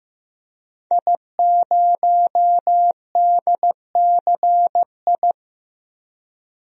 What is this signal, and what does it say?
Morse code, heard as a keyed tone, about 15 wpm, 700 Hz. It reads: I0DCI
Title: DXCC-100 Call Signs - 5 Characters - 15wpm